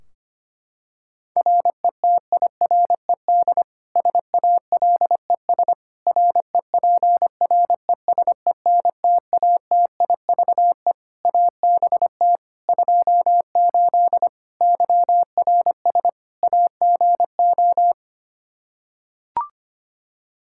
Morse code, heard as a keyed tone, about 25 wpm, 700 Hz. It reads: RETIRED SALES REPRESENTATIVE ABT 28 YRS AGO
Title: Practice QSO - Exchange 3 - More Info - 25wpm